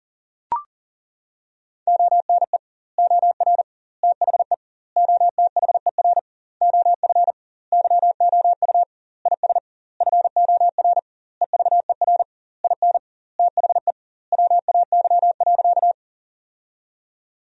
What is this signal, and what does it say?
Morse code, heard as a keyed tone, about 40 wpm, 700 Hz. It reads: ONE OR THE OTHER OF YOU IS FOR EVER IN THE WAY.
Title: 2x Word-Spacing - Sentences from Top 400 Words - 40wpm